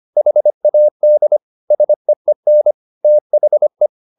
Morse code, heard as a keyed tone, about 25 wpm, 600 Hz. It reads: HAD SEEN THE